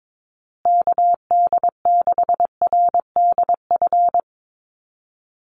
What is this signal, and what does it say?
Morse code, heard as a keyed tone, about 22 wpm, 700 Hz. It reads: KD6RDF
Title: USA Call Signs Repeated 3x 22wpm